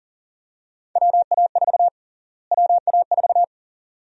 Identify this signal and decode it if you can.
Morse code, heard as a keyed tone, about 40 wpm, 700 Hz. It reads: WA4 WA4
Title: USA Call Sign Prefixes - Repeated 2x - 40wpm